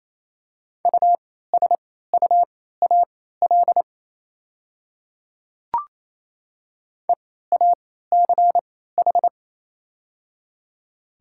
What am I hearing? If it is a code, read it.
Morse code, heard as a keyed tone, about 27 wpm, spaced as for 16 wpm, 690 Hz. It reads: USUAL EACH
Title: 3x Character Spacing - Top 500 Words - 28wpm